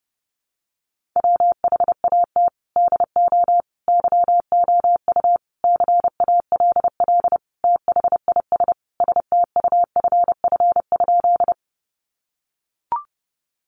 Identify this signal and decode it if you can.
Morse code, heard as a keyed tone, about 30 wpm, 700 Hz. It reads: WHAT DO YOU CALL THIS STUFF?
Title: Sentences from Top 1400 Words 30wpm